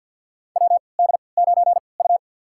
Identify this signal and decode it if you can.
Morse code, heard as a keyed tone, about 46 wpm, spaced as for 27 wpm, 700 Hz. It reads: WD9U
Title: USA Call Signs - 3x Character Spacing - 50wpm